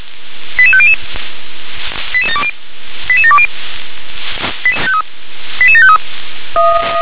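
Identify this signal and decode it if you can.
Five-tone selective call (ZVEI), heard as quick runs of five tones, tones of about 70 ms. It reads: 88500 88720 88753
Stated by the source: ZVEI five-tone call received by SDR.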